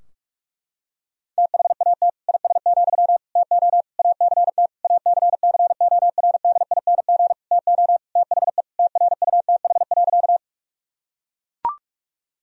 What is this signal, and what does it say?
Morse code, heard as a keyed tone, about 41 wpm, 700 Hz. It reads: THAT IS, TO ACT ACCORDING TO THE TRUTH.
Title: Sentences from Top 1000 Words - Review Part 4 - 45wpm